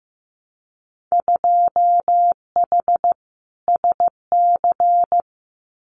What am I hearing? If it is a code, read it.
Morse code, heard as a keyed tone, about 15 wpm, 700 Hz. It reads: 2H SC
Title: Winter Field Day Exchanges 15wpm